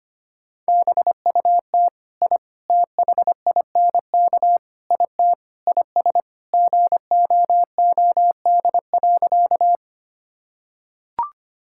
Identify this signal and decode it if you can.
Morse code, heard as a keyed tone, about 25 wpm, 700 Hz. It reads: BUT I THINK IT IS GOOD.
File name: practice / 25wpm / Sentences with 4-Word Phrases - 25wpm